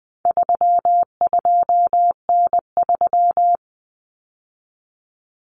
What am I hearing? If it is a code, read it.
Morse code, heard as a keyed tone, about 20 wpm, 700 Hz. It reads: 32N3
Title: Common Cut Numbers - 4-Digit 20wpm